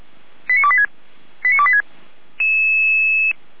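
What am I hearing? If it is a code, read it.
Five-tone selective call (ZVEI), heard as quick runs of five tones, tones of about 70 ms. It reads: 89286 89286
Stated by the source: ZVEI five-tone selective call received on an SDR.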